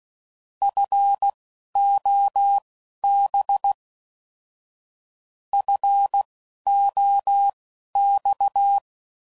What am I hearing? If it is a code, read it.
Morse code, heard as a keyed tone, about 16 wpm, spaced as for 11 wpm, 800 Hz. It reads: FOB FOX
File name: Prefixes / By First Letter / Fs